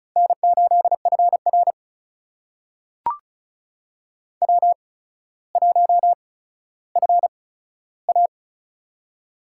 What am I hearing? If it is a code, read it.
Morse code, heard as a keyed tone, about 36 wpm, spaced as for 15 wpm, 700 Hz. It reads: N8FR W1FA
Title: USA Call Signs - 4 Characters - 8x Character Spacing - 35wpm